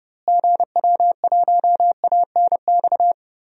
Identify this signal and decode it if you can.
Morse code, heard as a keyed tone, about 30 wpm, 700 Hz. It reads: GW1ANX
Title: DXCC-100 Call Signs - 6 Characters - Repeated 3x - 30wpm